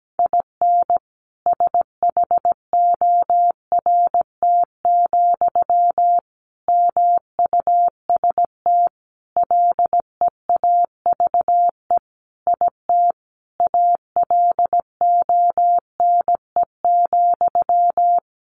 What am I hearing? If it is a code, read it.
Morse code, heard as a keyed tone, about 17 wpm, 700 Hz. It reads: IN SHORT, MUST LEAVE IT ALONE,
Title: Sentences from Top 1000 Words 17wpm